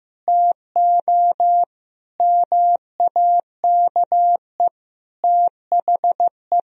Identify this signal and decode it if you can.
Morse code, heard as a keyed tone, about 15 wpm, 700 Hz. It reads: TO MAKE THE
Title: Sentences from Top 1000 Words - Review Part 1 - 15wpm